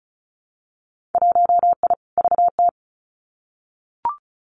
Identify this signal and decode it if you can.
Morse code, heard as a keyed tone, about 34 wpm, 700 Hz. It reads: 1I VT
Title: Winter Field Day Exchanges Encore 35wpm